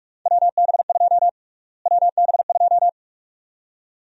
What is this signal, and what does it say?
Morse code, heard as a keyed tone, about 46 wpm, 700 Hz. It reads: WB2 WB2
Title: USA Call Sign Prefixes - Repeated 2x - 45wpm